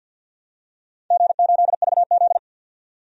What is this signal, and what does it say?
Morse code, heard as a keyed tone, about 51 wpm, 700 Hz. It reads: G8VZ
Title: DXCC-100 Call Signs - 4 Characters - 50wpm